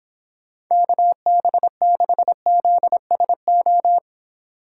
Morse code, heard as a keyed tone, about 25 wpm, 700 Hz. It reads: KB6ZSO